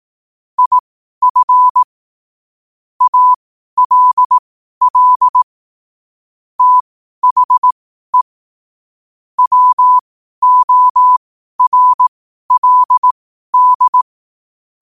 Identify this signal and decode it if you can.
Morse code, heard as a keyed tone, about 18 wpm, spaced as for 12 wpm, 1000 Hz. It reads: IF ALL THE WORLD
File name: qotdfiles / 12wpm